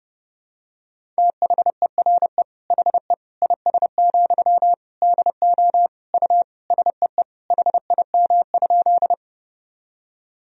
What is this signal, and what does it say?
Morse code, heard as a keyed tone, about 30 wpm, 700 Hz. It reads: THERE HE IS, DO U SEE HIM?
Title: Sentences from Top 100 Words and Abbreviations 30wpm